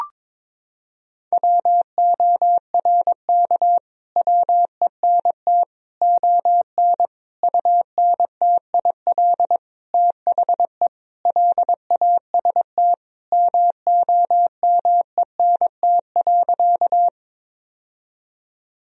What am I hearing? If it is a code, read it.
Morse code, heard as a keyed tone, about 22 wpm, 700 Hz. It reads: WORK WENT ON UNTIL THE LAST MOMENT.
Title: Sentences from Top 1000 Words - Review Part 1 - 22wpm